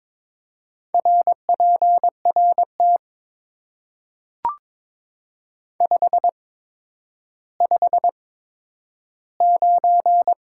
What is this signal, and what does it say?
Morse code, heard as a keyed tone, about 23 wpm, spaced as for 11 wpm, 700 Hz. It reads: RPRT 559
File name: Top 90 QSO Elements - 8x Character Spacing - 22wpm